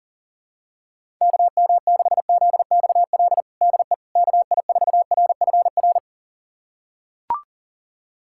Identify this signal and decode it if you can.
Morse code, heard as a keyed tone, about 40 wpm, 700 Hz. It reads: KM6ZXL DE KI4RFR
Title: USA Callsign DE Callsign 40wpm